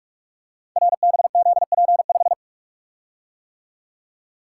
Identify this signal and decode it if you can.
Morse code, heard as a keyed tone, about 44 wpm, 700 Hz. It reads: RBZP5
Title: Code Groups - 5 Letter:Numbers - 45wpm